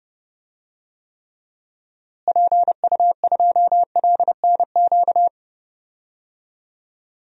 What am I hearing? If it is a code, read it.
Morse code, heard as a keyed tone, about 32 wpm, 700 Hz. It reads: PU2LNQ
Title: DXCC-100 Call Signs - 6 Characters - Repeated 2x - 30wpm